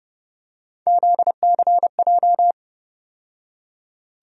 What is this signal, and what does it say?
Morse code, heard as a keyed tone, about 30 wpm, 700 Hz. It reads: ZCJ
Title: USA Call Sign Postfixes - 3 Letters 30wpm